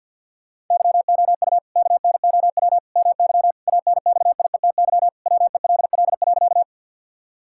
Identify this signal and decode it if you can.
Morse code, heard as a keyed tone, about 46 wpm, 700 Hz. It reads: YOU KNOW MY ANXIETY WELL.
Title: Sentences from Top 1600 Words 50wpm